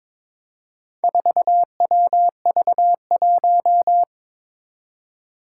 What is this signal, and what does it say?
Morse code, heard as a keyed tone, about 22 wpm, 700 Hz. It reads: 4WV1
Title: Four Challenging Letter-Number Combinations 22wpm